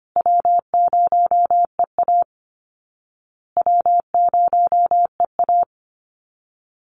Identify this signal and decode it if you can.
Morse code, heard as a keyed tone, about 25 wpm, 700 Hz. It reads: W0EA W0EA
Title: Easy Call Signs Repeated 3x - 25wpm